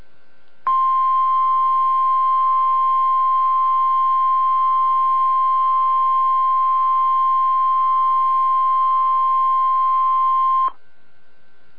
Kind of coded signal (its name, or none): Emergency Alert System (SAME) tones